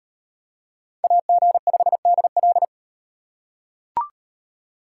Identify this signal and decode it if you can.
Morse code, heard as a keyed tone, about 38 wpm, 700 Hz. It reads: AG5DL